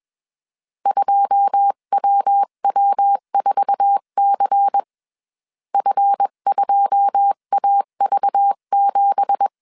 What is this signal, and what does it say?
Morse code, heard as a keyed tone, about 21 wpm, 800 Hz. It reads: 2WW4C F2AV7
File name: CW / DET / 1 9 1500C S B 20wpm 2313